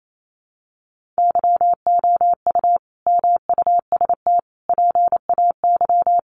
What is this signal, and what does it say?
Morse code, heard as a keyed tone, about 28 wpm, 700 Hz. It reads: YOU MUST PAY
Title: Sentences from Top 1400 Words 28wpm